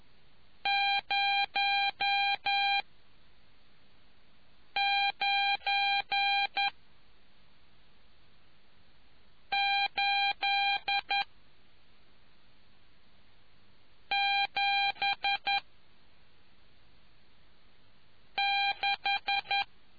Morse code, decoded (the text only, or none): 09876